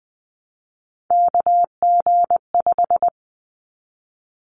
Morse code, heard as a keyed tone, about 20 wpm, 700 Hz. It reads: KG5
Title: Callsign Prefixes for Top 25 DXCC - 20wpm